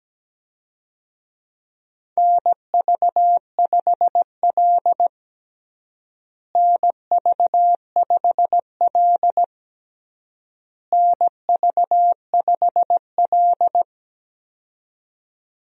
Morse code, heard as a keyed tone, about 17 wpm, 700 Hz. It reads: NV5L NV5L NV5L
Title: USA Call Signs - 4 Characters - Repeated 3x - 17wpm